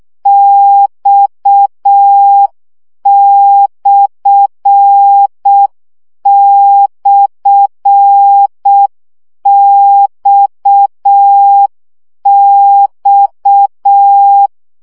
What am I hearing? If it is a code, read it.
Morse code, heard as a keyed tone, about 6 wpm, 800 Hz. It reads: X//XX
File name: CW gyakorlas x per 5 baud